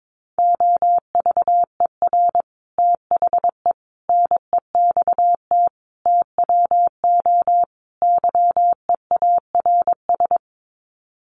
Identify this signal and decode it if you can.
Morse code, heard as a keyed tone, about 22 wpm, 700 Hz. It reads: OVER THE NEXT TWO YEARS
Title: Sets of 5 Words Encore2 22wpm